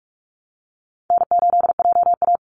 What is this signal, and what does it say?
Morse code, heard as a keyed tone, about 44 wpm, 700 Hz. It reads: N8JA
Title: Easy Call Signs Repeated 3x - 45wpm